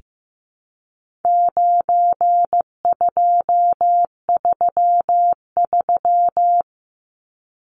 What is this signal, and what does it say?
Morse code, heard as a keyed tone, about 15 wpm, 700 Hz. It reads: 9233
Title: Numbers - 4-Digit 15wpm